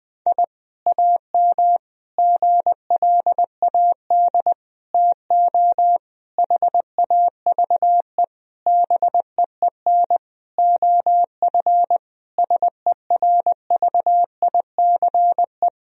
Morse code, heard as a keyed tone, about 21 wpm, 700 Hz. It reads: I AM GLAD TO HAVE BEEN OF SERVICE
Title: Sentences from Top 1000 Words - Review Part 4 - 20wpm